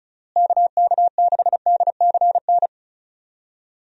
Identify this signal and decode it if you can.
Morse code, heard as a keyed tone, about 34 wpm, 700 Hz. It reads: KK6DCN
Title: USA Call Signs - 6 Characters - Repeated 3x - 35wpm